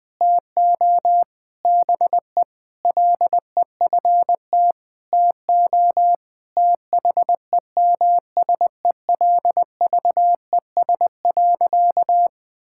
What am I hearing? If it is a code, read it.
Morse code, heard as a keyed tone, about 20 wpm, 700 Hz. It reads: TO BE LEFT TO THEMSELVES.
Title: Sentences from Top 1000 Words - Review Part 5 - 20wpm